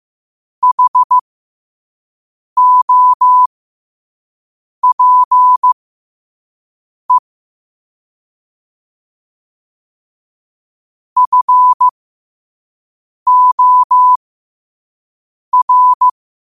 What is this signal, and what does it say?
Morse code, heard as a keyed tone, about 15 wpm, spaced as for 5 wpm, 1000 Hz. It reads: HOPE FOR
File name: qotdfiles / 5wpm